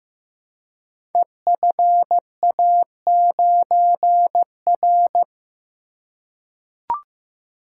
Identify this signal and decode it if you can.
Morse code, heard as a keyed tone, about 15 wpm, 700 Hz. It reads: EFA9R
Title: Code Groups - 5 Letter:Numbers - 15wpm